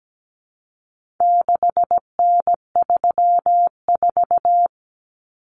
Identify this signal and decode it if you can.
Morse code, heard as a keyed tone, about 17 wpm, 700 Hz. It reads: 6N34
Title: Common Cut Numbers - 4-Digit 17wpm